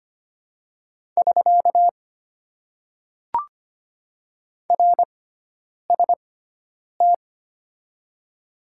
Morse code, heard as a keyed tone, about 25 wpm, 690 Hz. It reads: <SK> R S T
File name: Top 90 QSO Elements - 6x Character Spacing - 25wpm